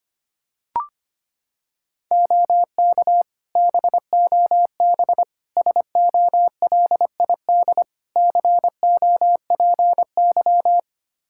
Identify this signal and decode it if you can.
Morse code, heard as a keyed tone, about 25 wpm, 700 Hz. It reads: OK BOB SOLID COPY
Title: Practice QSO - Exchange 3 - More Info - 25wpm